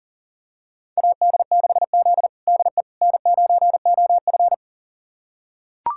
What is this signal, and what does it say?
Morse code, heard as a keyed tone, about 40 wpm, 700 Hz. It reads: AD6Z DE N9OF
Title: USA Callsign DE Callsign - 4 Characters - 40wpm